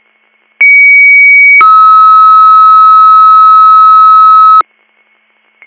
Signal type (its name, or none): two-tone page